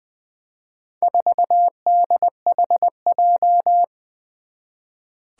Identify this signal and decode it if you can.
Morse code, heard as a keyed tone, about 20 wpm, 700 Hz. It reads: 4DHJ
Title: Four Challenging Letter-Number Combinations 20wpm